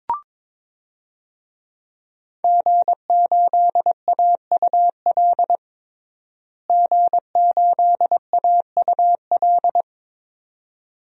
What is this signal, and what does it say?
Morse code, heard as a keyed tone, about 22 wpm, 700 Hz. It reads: G8AUL G8AUL
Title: DXCC-100 Call Signs - 5 Characters - Repeated 2x - 22wpm